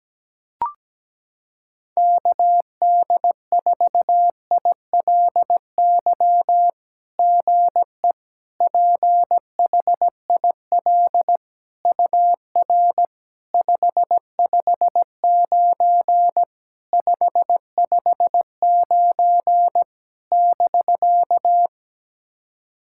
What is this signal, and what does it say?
Morse code, heard as a keyed tone, about 17 wpm, 700 Hz. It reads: KD4ILY GE PHIL UR 559 559 <BK>
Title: SOTA QSO Practice 17wpm